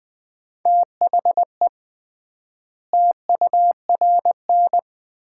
Morse code, heard as a keyed tone, about 21 wpm, 700 Hz. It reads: THE TURN
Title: 3x Word-Spacing - Sets of 4 Words - 20wpm